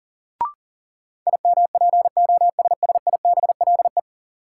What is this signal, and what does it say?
Morse code, heard as a keyed tone, about 39 wpm, 700 Hz. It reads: IMPOSSIBLE
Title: Sentences from Top 1100 Words - New Word Drill - 40wpm